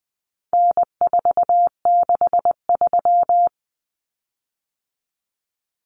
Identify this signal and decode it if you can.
Morse code, heard as a keyed tone, about 20 wpm, 700 Hz. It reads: N463
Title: Common Cut Numbers - 4-Digit 20wpm